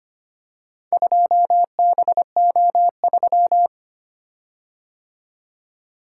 Morse code, heard as a keyed tone, about 25 wpm, 700 Hz. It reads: 2BO3
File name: practice / 25wpm / Four Letter-Number Combinations - 25wpm